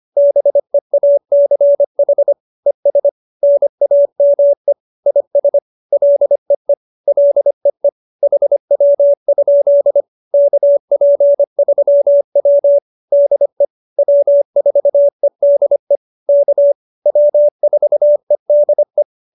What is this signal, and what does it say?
Morse code, heard as a keyed tone, about 25 wpm, 570 Hz. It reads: BEACH ES NAME IS LEE LEE HW? KP3W DE W4EDE K W4EDE